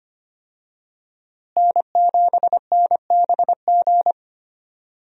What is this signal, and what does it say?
Morse code, heard as a keyed tone, about 25 wpm, 700 Hz. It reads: N7NBG
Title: USA Call Signs - 5 Characters - Repeated 2x - 25wpm